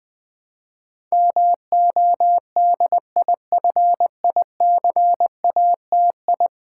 Morse code, heard as a keyed tone, about 20 wpm, 700 Hz. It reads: MODIFICATI
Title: Build A Word - 5 Letters At A Time - 12-Letter Words - 20wpm